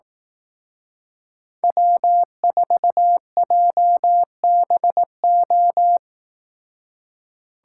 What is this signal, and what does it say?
Morse code, heard as a keyed tone, about 18 wpm, 700 Hz. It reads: W4JBO
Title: Speed Racing - USA 1x3 Call Signs 15wpm